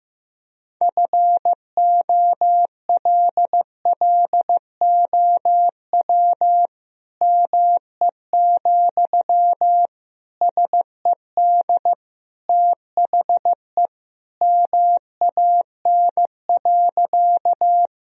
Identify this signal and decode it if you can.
Morse code, heard as a keyed tone, about 15 wpm, 700 Hz. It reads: FOLLOW ME, SED THE MAN.
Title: Sentences from Top 200 Words and Abbreviations 15wpm